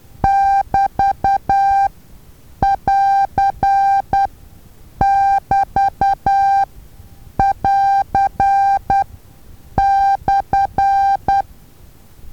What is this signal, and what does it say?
Morse code, heard as a keyed tone, about 10 wpm, spaced as for 7 wpm, 790 Hz. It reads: <BT><AR><BT><AR>/
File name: Taviro HA6VA A